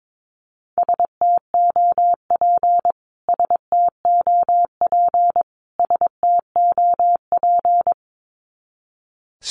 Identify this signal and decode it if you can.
Morse code, heard as a keyed tone, about 22 wpm, 700 Hz. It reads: STOP STOP STOP
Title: Top 300 Words Repeated 22wpm